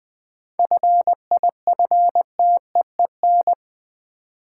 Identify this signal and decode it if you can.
Morse code, heard as a keyed tone, about 21 wpm, 700 Hz. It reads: FIFTEEN
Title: 7 Letter Words - First 3 Letters Given - 20wpm